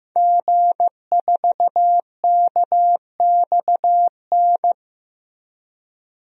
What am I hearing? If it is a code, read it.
Morse code, heard as a keyed tone, about 16 wpm, 700 Hz. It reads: G4KXN
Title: DXCC-100 Call Signs - 5 Characters - Repeated 2x - 15wpm